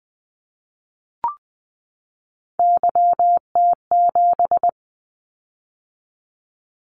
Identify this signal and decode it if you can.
Morse code, heard as a keyed tone, about 20 wpm, 700 Hz. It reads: YT7
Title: Callsign Prefixes for Top 25 DXCC - 20wpm